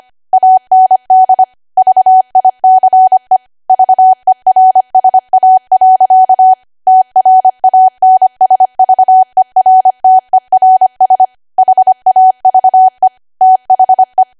Morse code, heard as a keyed tone, about 25 wpm, 750 Hz. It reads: AND VICE VERSA. TRANSVERTERS HAVE THE
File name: Archive / 25 WPM / 25WPM